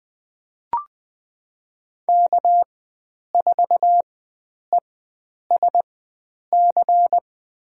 Morse code, heard as a keyed tone, about 20 wpm, spaced as for 9 wpm, 700 Hz. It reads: K4ESC
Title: USA Call Signs - 5 Characters - 4x Character Spacing - 20wpm